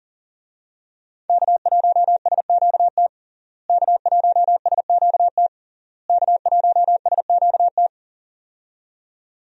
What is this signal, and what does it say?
Morse code, heard as a keyed tone, about 40 wpm, 700 Hz. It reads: K1SQT K1SQT K1SQT
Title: USA Call Signs - 5 Characters - Repeated 3x - 40wpm